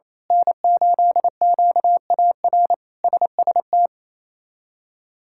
Morse code, heard as a keyed tone, about 28 wpm, 700 Hz. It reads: N8QAR SST
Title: K1USN SST QSO Practice - 28wpm